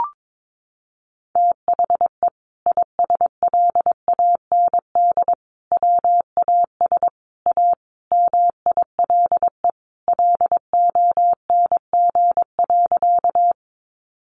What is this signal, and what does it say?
Morse code, heard as a keyed tone, about 22 wpm, 700 Hz. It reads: THE ISLAND WAS A MILE LONG.
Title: Sentences from Top 500 Words 22wpm